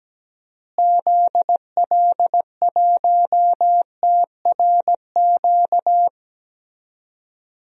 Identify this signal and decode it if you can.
Morse code, heard as a keyed tone, about 17 wpm, 700 Hz. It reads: ZL1TRQ
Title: DXCC-100 Call Signs - 6 Characters - Repeated 2x - 17wpm